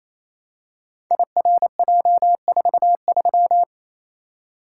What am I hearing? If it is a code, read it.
Morse code, heard as a keyed tone, about 29 wpm, 700 Hz. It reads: IRJ43
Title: Code Groups - 5 Letter:Numbers - 28wpm